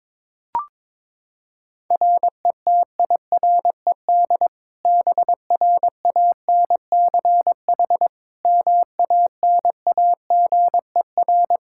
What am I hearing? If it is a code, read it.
Morse code, heard as a keyed tone, about 22 wpm, 700 Hz. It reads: RETIRED BRANCH MANAGER
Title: Practice QSO - Exchange 3 - More Info - 22wpm